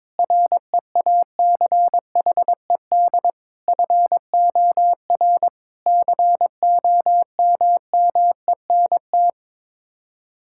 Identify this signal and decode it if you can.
Morse code, heard as a keyed tone, about 21 wpm, 700 Hz. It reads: REACHED FOR COMMENT
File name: practice / 22wpm / Sets of 7 Words - 22wpm